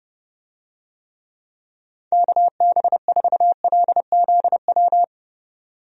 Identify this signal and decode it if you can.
Morse code, heard as a keyed tone, about 32 wpm, 700 Hz. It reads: KB4LZW
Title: USA Call Signs - 6 Characters - 30wpm